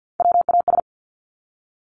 Morse code, heard as a keyed tone, about 50 wpm, 700 Hz. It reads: WAS
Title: Top Three Letter Combinations in Words 50wpm